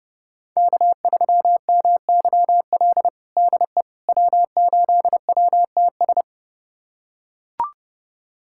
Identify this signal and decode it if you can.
Morse code, heard as a keyed tone, about 30 wpm, 700 Hz. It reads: K3MYL DE W8WTS
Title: USA Callsign DE Callsign - 5 Characters - 30wpm